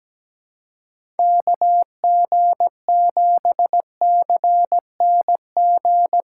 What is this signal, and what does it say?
Morse code, heard as a keyed tone, about 17 wpm, 700 Hz. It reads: KG7CNG
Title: USA Call Signs - 6 Characters - 3x Character Spacing - 17wpm